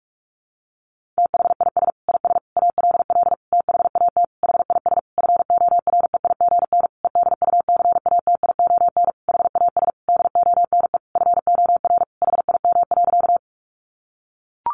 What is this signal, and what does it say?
Morse code, heard as a keyed tone, about 44 wpm, 700 Hz. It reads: THIS IS ALL THAT HIS FOREIGN EDUCATION HAS DONE FOR HIM.
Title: Sentences from Top 1400 Words 45wpm